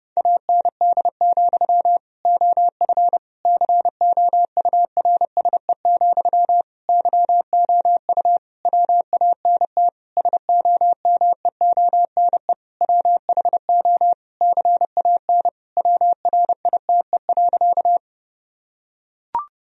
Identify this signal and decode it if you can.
Morse code, heard as a keyed tone, about 30 wpm, 700 Hz. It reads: AND, OF COURSE, YOU WANT SOMEONE WHO CAN WRITE.
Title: Sentences from Top 1000 Words - Review Part 1 - 30wpm